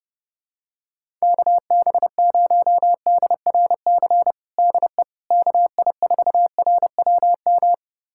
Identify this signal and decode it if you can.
Morse code, heard as a keyed tone, about 32 wpm, 700 Hz. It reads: KB0DRC DE KI4RWM
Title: USA Callsign DE Callsign - 6 Characters - 30wpm